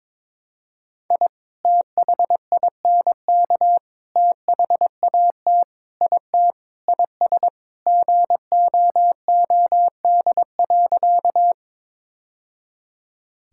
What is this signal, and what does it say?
Morse code, heard as a keyed tone, about 22 wpm, 700 Hz. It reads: I THINK THAT IT IS GOOD.
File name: practice / 22wpm / Sentences with 3-Word Phrases - 22wpm